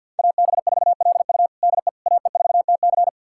Morse code, heard as a keyed tone, about 45 wpm, 700 Hz. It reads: AB4LU DE AE4TC
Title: USA Callsign DE Callsign - 5 Characters - 50wpm